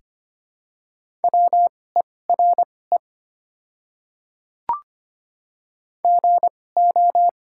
Morse code, heard as a keyed tone, about 25 wpm, spaced as for 18 wpm, 690 Hz. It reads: WERE GO
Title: 2x Character Spacing - Top 100 Words - 25wpm